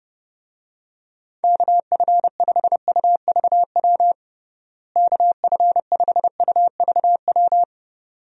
Morse code, heard as a keyed tone, about 30 wpm, 700 Hz. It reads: KF5UVW KF5UVW
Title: USA Call Signs - 6 Characters - Repeated 2x - 30wpm